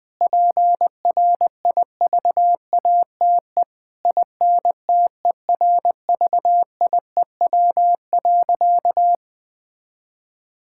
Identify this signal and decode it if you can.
Morse code, heard as a keyed tone, about 21 wpm, 700 Hz. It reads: PRIVATE INTERVIEW.
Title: Sentences from Top 1600 Words 20wpm